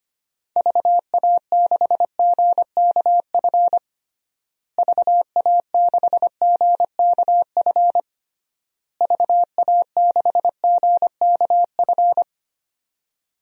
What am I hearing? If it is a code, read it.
Morse code, heard as a keyed tone, about 25 wpm, 700 Hz. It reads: VA6GKF VA6GKF VA6GKF
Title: DXCC-100 Call Signs - 6 Characters - Repeated 3x - 25wpm